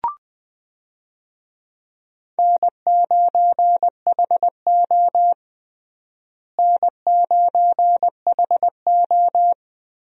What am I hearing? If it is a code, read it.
Morse code, heard as a keyed tone, about 20 wpm, 700 Hz. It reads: N9HO N9HO
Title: USA Call Signs - 4 Characters - Repeated 2x - 20wpm